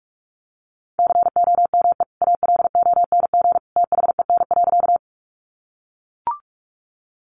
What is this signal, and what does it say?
Morse code, heard as a keyed tone, about 44 wpm, 700 Hz. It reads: COME ALONG THEN.
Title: Sentences from Top 700 Words 45wpm